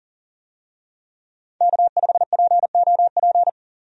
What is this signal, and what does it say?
Morse code, heard as a keyed tone, about 40 wpm, 700 Hz. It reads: K5POP
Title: USA Call Signs - 5 Characters - 40wpm